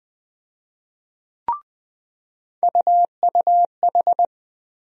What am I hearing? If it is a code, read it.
Morse code, heard as a keyed tone, about 20 wpm, 700 Hz. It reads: UUH
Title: USA Call Sign Postfixes - 3 Letters 20wpm